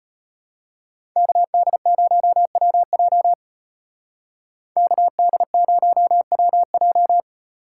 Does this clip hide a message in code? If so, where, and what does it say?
Yes: Morse code from 1.2 to 7.2 s, heard as a keyed tone, about 33 wpm, 700 Hz. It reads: KD0WJ KD0WJ